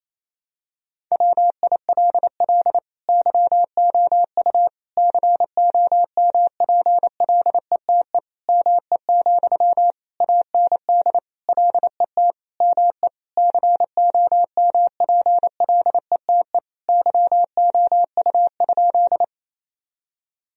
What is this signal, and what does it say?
Morse code, heard as a keyed tone, about 28 wpm, 700 Hz. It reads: WILL YOU COMPLETE ME, AND LET ME COMPLETE YOU?
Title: Sentences from Top 1000 Words - Review Part 2 - 28wpm